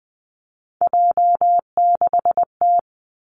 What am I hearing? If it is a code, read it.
Morse code, heard as a keyed tone, about 20 wpm, 700 Hz. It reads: J6T
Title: Three Challenging Letter-Number Combinations 20wpm